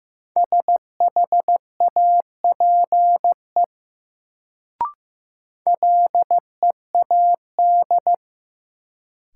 Morse code, heard as a keyed tone, about 15 wpm, 700 Hz. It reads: SHAPE LEAD
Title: Sentences from Top 500 Words - New Word Drill - 15wpm